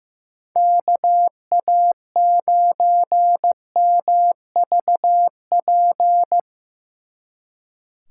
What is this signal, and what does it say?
Morse code, heard as a keyed tone, about 17 wpm, 700 Hz. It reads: KA9MVP
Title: Speed Racing - USA 2x3 Call Signs 15wpm